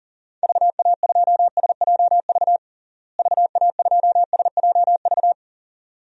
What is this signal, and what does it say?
Morse code, heard as a keyed tone, about 37 wpm, 700 Hz. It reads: VA2SJV VA2SJV
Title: DXCC-100 Call Signs - 6 Characters - Repeated 2x - 40wpm